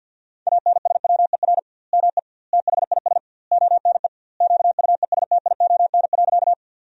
Morse code, heard as a keyed tone, about 50 wpm, 700 Hz. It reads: ANSWER ME THIS ONE QUESTION.
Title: 2x Word-Spacing - Sentences from Top 400 Words - 50wpm